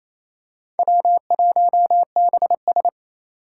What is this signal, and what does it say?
Morse code, heard as a keyed tone, about 27 wpm, 700 Hz. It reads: W1BS
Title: Four Challenging Letter-Number Combinations 28wpm